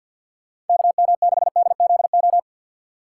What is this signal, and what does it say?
Morse code, heard as a keyed tone, about 47 wpm, 700 Hz. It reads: KM6DZO
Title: USA Call Signs - 6 Characters - 3x Character Spacing - 50wpm